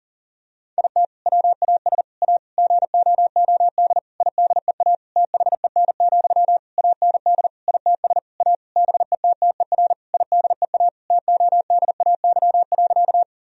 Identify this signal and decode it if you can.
Morse code, heard as a keyed tone, about 40 wpm, 700 Hz. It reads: IT WAS A GOOD IDEA THEN, AND ITS A BETTER IDEA TODAY.
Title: Sentences from Top 1000 Words - Review Part 4 - 40wpm